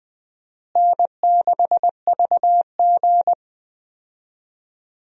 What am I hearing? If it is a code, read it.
Morse code, heard as a keyed tone, about 19 wpm, 700 Hz. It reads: N6VG
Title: USA Call Signs - 4 Characters - Repeated 3x - 20wpm